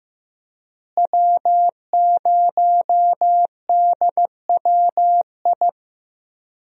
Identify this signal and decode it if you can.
Morse code, heard as a keyed tone, about 15 wpm, 700 Hz. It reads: W0DWI
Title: USA Call Signs - 5 Characters - Repeated 2x - 15wpm